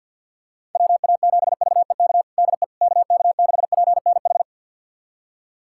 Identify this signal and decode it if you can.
Morse code, heard as a keyed tone, about 52 wpm, 700 Hz. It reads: WA7VEK DE KK6PNH
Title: USA Callsign DE Callsign - 6 Characters - 50wpm